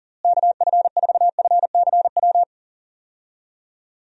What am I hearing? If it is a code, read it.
Morse code, heard as a keyed tone, about 37 wpm, 700 Hz. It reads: KF4FCW
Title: USA Call Signs - 6 Characters - Repeated 3x - 40wpm